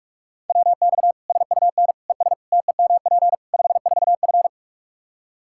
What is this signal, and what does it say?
Morse code, heard as a keyed tone, about 46 wpm, 700 Hz. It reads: WX SUN ES TEMP 54F
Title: Practice QSO - Exchange 2 - Info - 45wpm